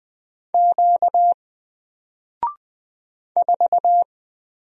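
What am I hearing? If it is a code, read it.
Morse code, heard as a keyed tone, about 20 wpm, 700 Hz. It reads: Q 4
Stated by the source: Single letter and number Morse drill at 20 wpm, from N4LCD.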